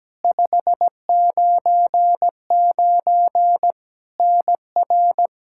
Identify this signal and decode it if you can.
Morse code, heard as a keyed tone, about 17 wpm, 700 Hz. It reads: 599 NR
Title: Practice QSO - Exchange 1 - Beginning - 17wpm